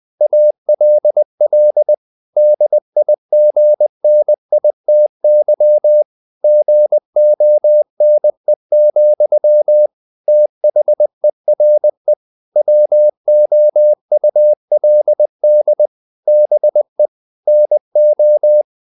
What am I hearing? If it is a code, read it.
Morse code, heard as a keyed tone, about 20 wpm, 600 Hz. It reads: ALL DIGNITY GONE, THERE WOULD BE NO
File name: the old man and the sea part 2 20wpm